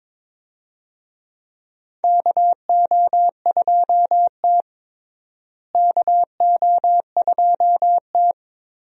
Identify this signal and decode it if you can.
Morse code, heard as a keyed tone, about 22 wpm, 700 Hz. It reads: KO2T KO2T
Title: USA Call Signs - 4 Characters - Repeated 3x - 22wpm